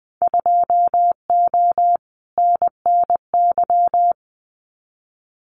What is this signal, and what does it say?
Morse code, heard as a keyed tone, about 20 wpm, 700 Hz. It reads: 2O NNY
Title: Winter Field Day Exchanges 20wpm